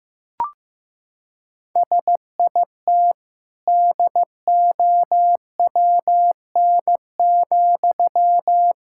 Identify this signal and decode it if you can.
Morse code, heard as a keyed tone, about 15 wpm, 700 Hz. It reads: SIT DOWN,
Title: Sentences from Top 1000 Words - Review Part 6 - 15wpm